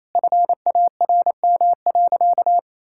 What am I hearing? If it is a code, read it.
Morse code, heard as a keyed tone, about 28 wpm, 700 Hz. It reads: FARM.
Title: Sentences from Top 1000 Words - Review Part 6 - 28wpm